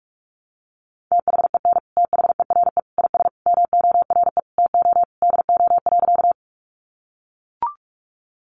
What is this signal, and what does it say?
Morse code, heard as a keyed tone, about 44 wpm, 700 Hz. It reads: THEN THERE IS MORE TO DO.
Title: Sentences from Top 100 Words 45wpm